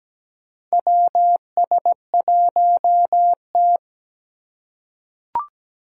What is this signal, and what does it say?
Morse code, heard as a keyed tone, about 17 wpm, 700 Hz. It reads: WS1T
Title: USA Call Signs - 2x Character Spacing - 17wpm